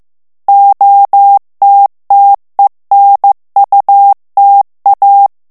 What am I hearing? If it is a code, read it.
Morse code, heard as a keyed tone, about 15 wpm, 800 Hz. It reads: OTTENUTA
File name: morse mp3 / ITALIEN texte clair 2 vitesse 14 mots minute